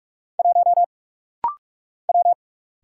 Morse code, heard as a keyed tone, about 45 wpm, 700 Hz. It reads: 1 W
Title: Single Letter-Number Alt - Rapid-Fire 45wpm